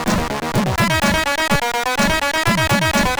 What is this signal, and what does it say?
There is no code.